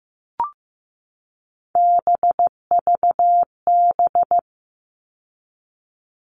Morse code, heard as a keyed tone, about 15 wpm, 700 Hz. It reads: BVB
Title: All Cut Numbers - 3-Digit 15wpm